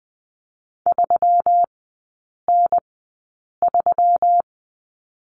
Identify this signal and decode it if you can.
Morse code, heard as a keyed tone, about 20 wpm, spaced as for 8 wpm, 700 Hz. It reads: 3N3
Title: 121 Lesson 20e - Number 3 - Three Characters 20wpm